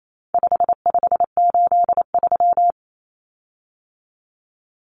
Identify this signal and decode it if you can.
Morse code, heard as a keyed tone, about 28 wpm, 700 Hz. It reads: 5583
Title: Common Cut Numbers - 4-Digit 28wpm